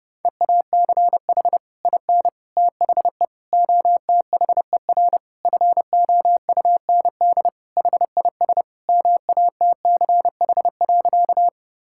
Morse code, heard as a keyed tone, about 30 wpm, 700 Hz. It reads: EACH IN THE OTHER FOUND HIS MATCH.
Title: Sentences from Top 1600 Words 30wpm